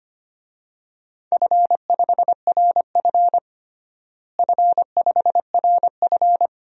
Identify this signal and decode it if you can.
Morse code, heard as a keyed tone, about 25 wpm, 700 Hz. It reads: F5RF F5RF
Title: DXCC-100 Call Signs - 4 Characters - Repeated 2x - 25wpm